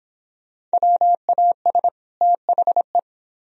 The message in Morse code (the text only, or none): WAS THE